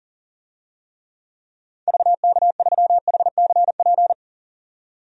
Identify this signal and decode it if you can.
Morse code, heard as a keyed tone, about 37 wpm, 700 Hz. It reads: VK3HCP
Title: DXCC-100 Call Signs - 6 Characters - 40wpm